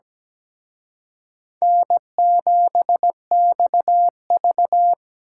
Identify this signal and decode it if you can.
Morse code, heard as a keyed tone, about 20 wpm, 700 Hz. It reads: N7XV
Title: Speed Racing - USA Call Signs 17wpm